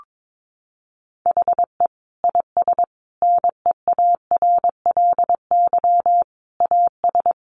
Morse code, heard as a keyed tone, about 22 wpm, 700 Hz. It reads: HE IS NEARLY AS
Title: Sentences from Top 1300 Words 22wpm